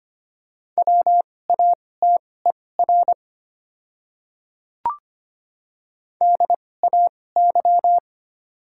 Morse code, heard as a keyed tone, about 25 wpm, spaced as for 18 wpm, 700 Hz. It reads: WATER DAY
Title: 2x Character Spacing - Top 200 Words - 25wpm